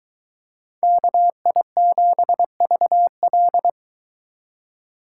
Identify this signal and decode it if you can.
Morse code, heard as a keyed tone, about 23 wpm, 700 Hz. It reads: KI7VL